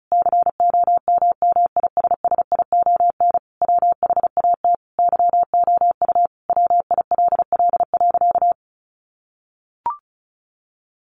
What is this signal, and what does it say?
Morse code, heard as a keyed tone, about 34 wpm, 700 Hz. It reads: COMMISSION WHAT YOU WILL.
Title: Sentences from Top 1400 Words 35wpm